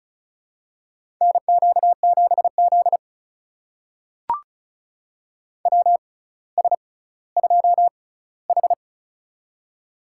Morse code, heard as a keyed tone, about 36 wpm, spaced as for 18 wpm, 700 Hz. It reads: NQ7Z WS2H
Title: USA Call Signs - 6x Character Spacing - 35wpm